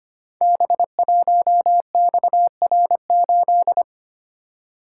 Morse code, heard as a keyed tone, about 25 wpm, 690 Hz. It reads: B1XR8
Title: Code Groups - 5 Letter:Numbers - 25wpm